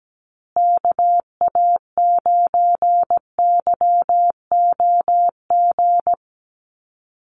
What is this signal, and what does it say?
Morse code, heard as a keyed tone, about 17 wpm, 700 Hz. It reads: KA9YOG
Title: USA Call Signs Repeated 2x 17wpm